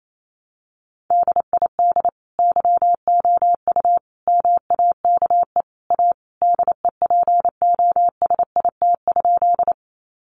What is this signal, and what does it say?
Morse code, heard as a keyed tone, about 28 wpm, 700 Hz. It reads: DID YOU MAKE A DEPOSIT?
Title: Sentences from Top 1300 Words 28wpm